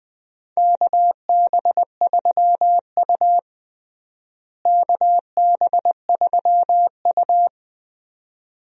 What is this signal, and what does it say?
Morse code, heard as a keyed tone, about 20 wpm, 700 Hz. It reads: KB3U KB3U
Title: USA Call Signs - 4 Characters - Repeated 2x - 20wpm